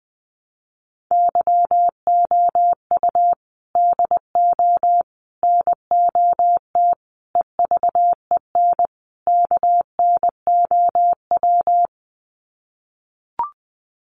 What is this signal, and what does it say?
Morse code, heard as a keyed tone, about 20 wpm, 700 Hz. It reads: YOU DO NOT EVEN KNOW
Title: Sets of 5 Words Encore2 20wpm